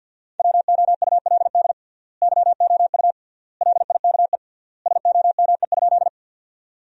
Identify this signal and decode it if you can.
Morse code, heard as a keyed tone, about 50 wpm, 700 Hz. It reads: WOULD YOU LIKE SOME?
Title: 3x Word-Spacing - Sentences from Top 100 Words - 50wpm